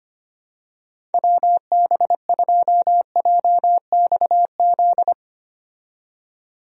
Morse code, heard as a keyed tone, about 25 wpm, 700 Hz. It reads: WB2JXZ
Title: USA Call Signs - 6 Characters - 25wpm